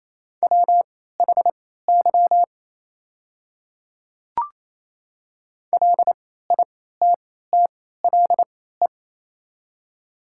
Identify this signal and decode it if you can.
Morse code, heard as a keyed tone, about 28 wpm, spaced as for 16 wpm, 690 Hz. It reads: WHY LITTLE
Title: 3x Character Spacing - Top 200 Words - 28wpm